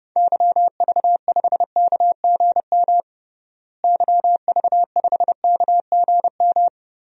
Morse code, heard as a keyed tone, about 30 wpm, 700 Hz. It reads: YV5KGM YV5KGM
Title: DXCC-100 Call Signs - 6 Characters - Repeated 3x - 30wpm